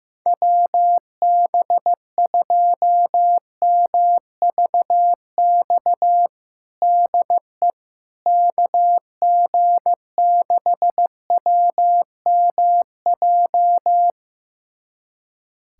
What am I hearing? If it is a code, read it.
Morse code, heard as a keyed tone, about 15 wpm, 700 Hz. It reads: WB2MVX DE KG6WMJ
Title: USA Callsign DE Callsign - 6 Characters - 15wpm